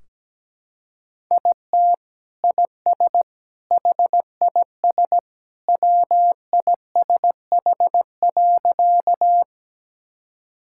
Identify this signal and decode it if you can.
Morse code, heard as a keyed tone, about 17 wpm, 700 Hz. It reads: IT IS HIS WISH.
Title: Sentences from Top 1600 Words 17wpm